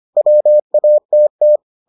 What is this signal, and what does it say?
Morse code, heard as a keyed tone, about 25 wpm, 600 Hz. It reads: WATT